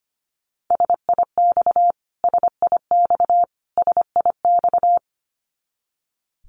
Morse code, heard as a keyed tone, about 25 wpm, 700 Hz. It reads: SIX SIX SIX
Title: Top 500 Words Repeated 25wpm